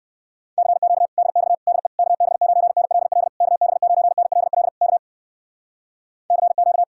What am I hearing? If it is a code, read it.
Morse code, heard as a keyed tone, about 58 wpm, 700 Hz. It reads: CQ DX DE KB9NLL KB9NLL K CQ